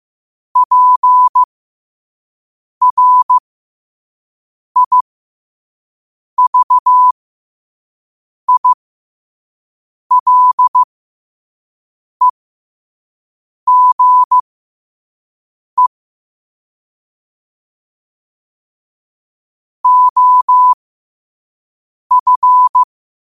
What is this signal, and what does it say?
Morse code, heard as a keyed tone, about 15 wpm, spaced as for 5 wpm, 1000 Hz. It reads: PRIVILEGE OF
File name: qotdfiles / 5wpm